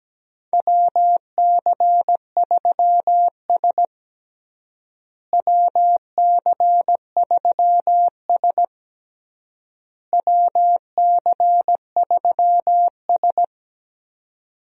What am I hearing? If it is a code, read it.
Morse code, heard as a keyed tone, about 17 wpm, 700 Hz. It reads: WC3S WC3S WC3S
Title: USA Call Signs - 4 Characters - Repeated 3x - 17wpm